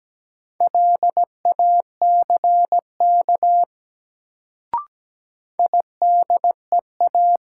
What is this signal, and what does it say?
Morse code, heard as a keyed tone, about 17 wpm, 700 Hz. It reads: LACK IDEA
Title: Sets of 3 Words - New Word Drill - 17wpm